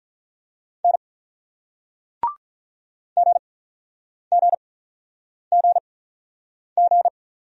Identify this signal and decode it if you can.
Morse code, heard as a keyed tone, about 46 wpm, spaced as for 9 wpm, 700 Hz. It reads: N GGGG